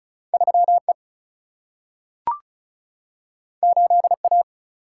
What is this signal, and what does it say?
Morse code, heard as a keyed tone, about 36 wpm, 700 Hz. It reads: 3E 8A
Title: Single-Letter Single-Number 35wpm